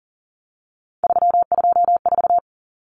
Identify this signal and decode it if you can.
Morse code, heard as a keyed tone, about 40 wpm, 700 Hz. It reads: 324
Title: Common Cut Numbers - 3-Digit 40wpm